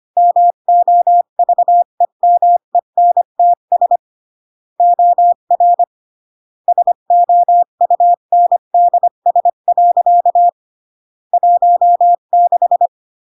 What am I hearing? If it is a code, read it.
Morse code, heard as a keyed tone, about 25 wpm, 700 Hz. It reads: MOVEMENTS OR SOUNDS. 16